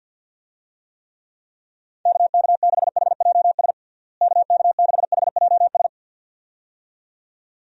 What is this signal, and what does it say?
Morse code, heard as a keyed tone, about 47 wpm, 700 Hz. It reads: KK6HJS KK6HJS
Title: USA Call Signs - 6 Characters - Repeated 2x - 50wpm